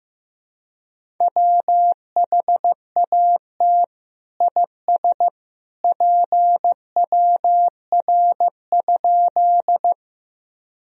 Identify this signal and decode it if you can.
Morse code, heard as a keyed tone, about 15 wpm, 700 Hz. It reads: WHAT IS PWR?
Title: Sentences from Top 500 Words and Abbreviations 15wpm